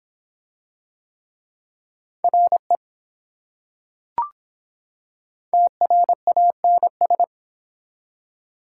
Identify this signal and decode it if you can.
Morse code, heard as a keyed tone, about 24 wpm, 700 Hz. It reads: RE TRANS